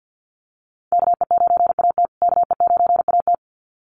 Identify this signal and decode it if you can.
Morse code, heard as a keyed tone, about 50 wpm, 700 Hz. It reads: KE9AT KE9AT
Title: USA Call Signs Repeated 2x 50wpm